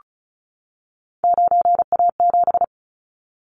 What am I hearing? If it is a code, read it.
Morse code, heard as a keyed tone, about 34 wpm, 700 Hz. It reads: 9A7
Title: Callsign Prefixes for Top 50 DXCC - 35wpm